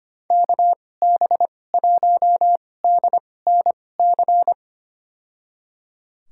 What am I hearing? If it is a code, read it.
Morse code, heard as a keyed tone, about 25 wpm, spaced as for 18 wpm, 700 Hz. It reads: KB1DNC
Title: USA Call Signs - 6 Characters - 2x Character Spacing - 25wpm